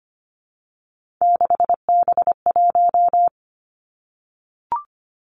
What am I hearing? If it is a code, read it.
Morse code, heard as a keyed tone, about 25 wpm, 700 Hz. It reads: 6B1
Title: Three Challenging Letter-Number Combinations 25wpm